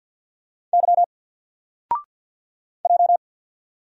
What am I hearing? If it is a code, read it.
Morse code, heard as a keyed tone, about 48 wpm, 700 Hz. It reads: Y J
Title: Single Letter-Number Alt 50wpm